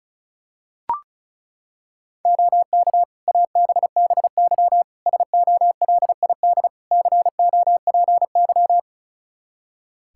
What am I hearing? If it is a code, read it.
Morse code, heard as a keyed tone, about 36 wpm, 700 Hz. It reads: OK ABBY SOLID COPY
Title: Practice QSO - Exchange 3 - More Info - 35wpm